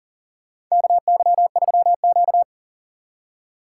Morse code, heard as a keyed tone, about 40 wpm, 700 Hz. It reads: KY3Q
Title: USA Call Signs - 2x Character Spacing - 40wpm